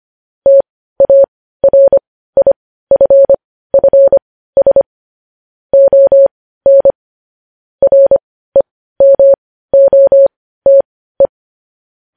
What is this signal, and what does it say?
Morse code, heard as a keyed tone, about 25 wpm, spaced as for 15 wpm, 550 Hz. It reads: TARIFFS ON REMOTE
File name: podcast / News headlines-1x-25x15